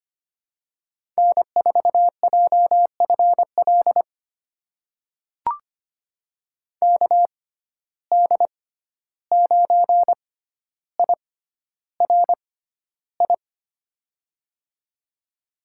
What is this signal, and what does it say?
Morse code, heard as a keyed tone, about 25 wpm, spaced as for 12 wpm, 700 Hz. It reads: N4JFL KD9IRI
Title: USA Call Signs - 6x Character Spacing - 25wpm